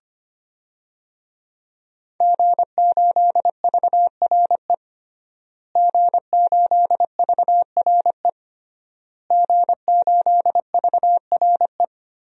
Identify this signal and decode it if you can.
Morse code, heard as a keyed tone, about 25 wpm, 700 Hz. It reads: G8VRE G8VRE G8VRE
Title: DXCC-100 Call Signs - 5 Characters - Repeated 3x - 25wpm